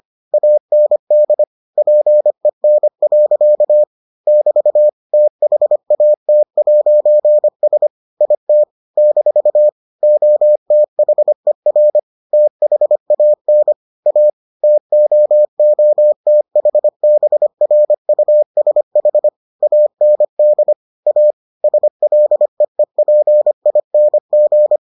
Morse code, heard as a keyed tone, about 25 wpm, 600 Hz. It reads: AND PEN. <BT> THAT'S IT - OTHER THAN A TOOTHBRUSH AND A SLEEPING